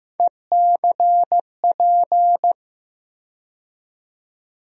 Morse code, heard as a keyed tone, about 15 wpm, 700 Hz. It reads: ECP
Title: USA Call Sign Postfixes - 3 Letters 15wpm